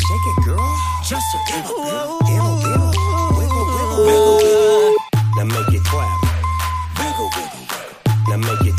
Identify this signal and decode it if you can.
Ringback tone (Europe, 425 Hz), with noise.